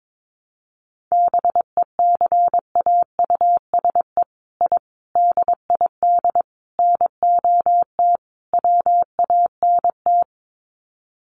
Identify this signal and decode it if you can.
Morse code, heard as a keyed tone, about 22 wpm, 700 Hz. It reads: BECAUSE I DID NOT WANT
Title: Sets of 5 Words Encore2 22wpm